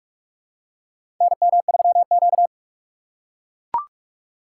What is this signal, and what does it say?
Morse code, heard as a keyed tone, about 43 wpm, 700 Hz. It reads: NM3Q
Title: USA Call Signs - 2x Character Spacing - 45wpm